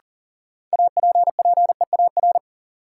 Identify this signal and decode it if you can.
Morse code, heard as a keyed tone, about 40 wpm, 700 Hz. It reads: APPEAR
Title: Sentences from Top 1200 Words - New Word Drill Alt - 40wpm